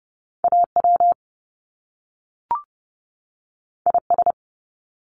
Morse code, heard as a keyed tone, about 30 wpm, 700 Hz. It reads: AW IS
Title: Top Two Letter Combinations in Words 30wpm